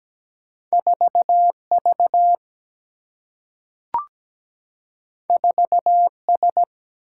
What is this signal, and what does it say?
Morse code, heard as a keyed tone, about 17 wpm, 700 Hz. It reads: 4 V 4 S
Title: Single-Letter Single-Number 17wpm